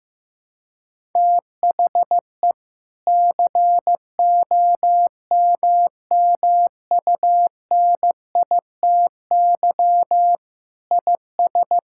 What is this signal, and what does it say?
Morse code, heard as a keyed tone, about 15 wpm, 700 Hz. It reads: THE COMMUNITY IS
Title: Sentences from Top 1000 Words - Review Part 6 - 15wpm